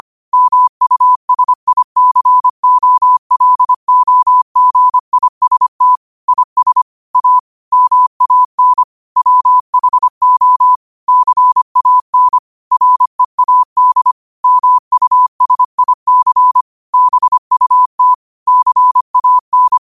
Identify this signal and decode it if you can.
Morse code, heard as a keyed tone, about 25 wpm, 1000 Hz. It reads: MUSICOLOGIST IS A MAN WHO CAN READ MUSIC BUT CAN